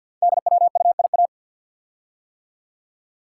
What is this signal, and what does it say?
Morse code, heard as a keyed tone, about 47 wpm, 700 Hz. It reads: DWUIA
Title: Code Groups - 5 Letter:Numbers - 50wpm